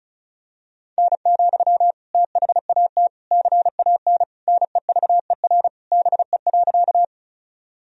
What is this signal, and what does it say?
Morse code, heard as a keyed tone, about 36 wpm, 700 Hz. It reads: N, THAT CAN NEVER BE.
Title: Sentences from Top 300 Words and Abbreviations 35wpm